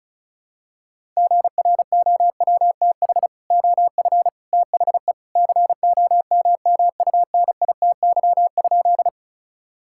Morse code, heard as a keyed tone, about 36 wpm, 700 Hz. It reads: GROWTH OF THE COMMUNITY?
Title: Sentences from Top 1000 Words - Review Part 5 - 35wpm